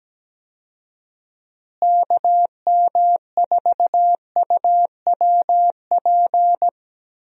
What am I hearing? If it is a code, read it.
Morse code, heard as a keyed tone, about 17 wpm, 700 Hz. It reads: KM4UWP
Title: USA Call Signs - 6 Characters - Repeated 3x - 17wpm